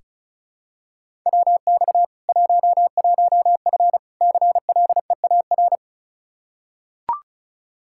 Morse code, heard as a keyed tone, about 36 wpm, 700 Hz. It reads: WX 11F CLEAR
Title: Practice QSO - Exchange 2 - Info - 35wpm